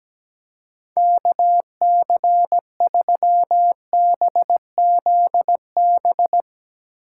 Morse code, heard as a keyed tone, about 17 wpm, 700 Hz. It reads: KC3BZB
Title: USA Call Signs - 6 Characters - 6x Character Spacing - 17wpm